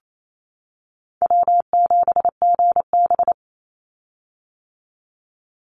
Morse code, heard as a keyed tone, about 28 wpm, 700 Hz. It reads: W7GB
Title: Call Signs - 28wpm